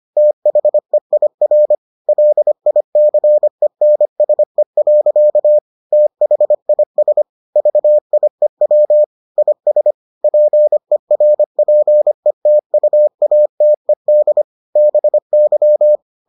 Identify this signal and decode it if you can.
Morse code, heard as a keyed tone, about 25 wpm, 600 Hz. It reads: THEIR LICENSE. THIS VIEW IS PERPETUATED BY